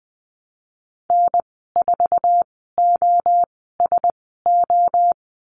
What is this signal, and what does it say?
Morse code, heard as a keyed tone, about 20 wpm, spaced as for 15 wpm, 700 Hz. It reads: N4OSO
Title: 055 Lesson 10g - Letter r - Call Signs 20wpm